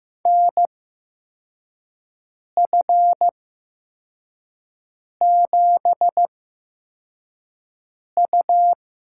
Morse code, heard as a keyed tone, about 15 wpm, spaced as for 4 wpm, 700 Hz. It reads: NF7U
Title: USA Call Signs - 8x Character Spacing - 15wpm